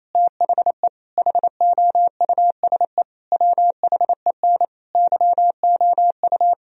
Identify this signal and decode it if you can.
Morse code, heard as a keyed tone, about 28 wpm, 700 Hz. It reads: THE HOUSE WHEN YOU
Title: Sentences from Top 1000 Words - Review Part 2 - 28wpm